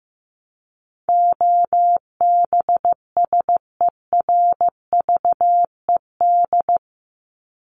Morse code, heard as a keyed tone, about 15 wpm, 700 Hz. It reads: OBSERVED
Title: ED Suffix - Top 100 Words with Base Words 15wpm